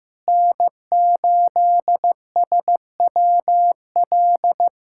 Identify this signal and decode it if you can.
Morse code, heard as a keyed tone, about 15 wpm, 700 Hz. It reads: N8SWL
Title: USA Callsign DE Callsign 15wpm